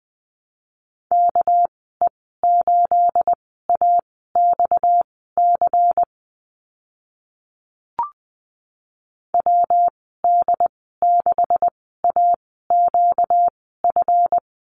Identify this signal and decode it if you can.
Morse code, heard as a keyed tone, about 20 wpm, spaced as for 14 wpm, 700 Hz. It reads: KE8AXC WD6AQF
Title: 245 Lesson 39g - Letter z - Call Signs 20wpm